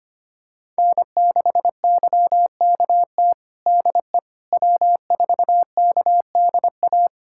Morse code, heard as a keyed tone, about 25 wpm, 700 Hz. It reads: N6YKT DE W4KDA
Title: USA Callsign DE Callsign - 5 Characters - 25wpm